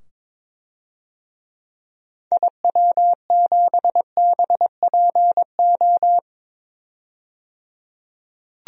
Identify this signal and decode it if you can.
Morse code, heard as a keyed tone, about 21 wpm, 700 Hz. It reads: IW7BPO
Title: DXCC-100 Call Signs - 6 Characters - 22wpm